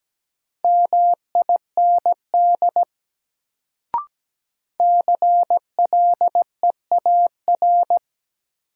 Morse code, heard as a keyed tone, about 17 wpm, 700 Hz. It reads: MIND CLEAR
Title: Sentences from Top 500 Words - New Word Drill - 17wpm